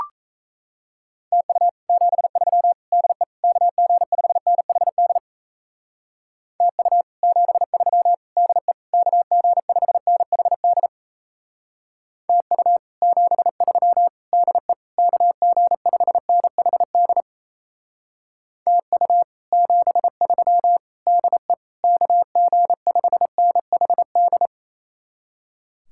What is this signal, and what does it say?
Morse code, heard as a keyed tone, about 38 wpm, 700 Hz. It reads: TU 73 DE KG5NHD TU 73 DE KG5NHD TU 73 DE KG5NHD TU 73 DE KG5NHD